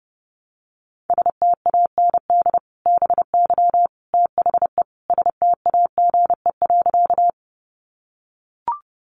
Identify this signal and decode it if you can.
Morse code, heard as a keyed tone, about 30 wpm, 700 Hz. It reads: STAND BY THE STAGE.
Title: Sentences from Top 1200 Words 30wpm